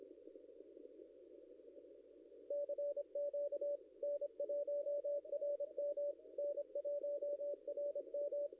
Morse code, heard as a keyed tone, about 25 wpm, 560 Hz. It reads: CQ N1RM N1RM
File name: contests / ss-cw-k5tr-op-n5rz -0700